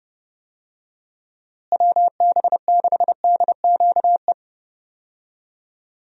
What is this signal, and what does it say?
Morse code, heard as a keyed tone, about 30 wpm, 700 Hz. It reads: WB6DQE
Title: USA Call Signs - 6 Characters - Repeated 2x - 30wpm